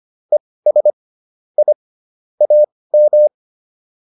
Morse code, heard as a keyed tone, about 25 wpm, spaced as for 18 wpm, 600 Hz. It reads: ES I AM
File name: PR103 18